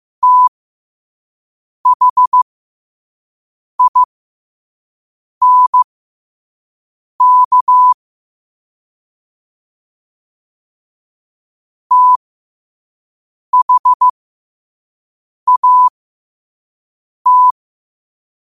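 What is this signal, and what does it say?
Morse code, heard as a keyed tone, about 15 wpm, spaced as for 5 wpm, 1000 Hz. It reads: THINK THAT
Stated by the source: Morse code quote of the day at 5 wpm.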